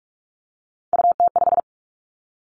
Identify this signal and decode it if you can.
Morse code, heard as a keyed tone, about 44 wpm, 700 Hz. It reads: UT5
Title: Callsign Prefixes for Top 25 DXCC - Repeated 3x - 45wpm